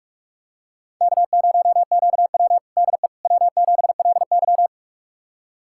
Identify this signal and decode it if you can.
Morse code, heard as a keyed tone, about 42 wpm, 700 Hz. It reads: K0QW DE W7LY
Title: USA Callsign DE Callsign - 4 Characters - 45wpm